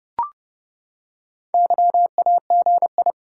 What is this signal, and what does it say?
Morse code, heard as a keyed tone, about 30 wpm, 700 Hz. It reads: YAGI
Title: Top 180 QSO Elements 30wpm